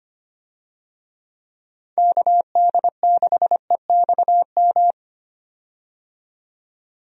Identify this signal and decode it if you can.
Morse code, heard as a keyed tone, about 25 wpm, 700 Hz. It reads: KD6EXM
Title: USA Call Signs - 6 Characters - 25wpm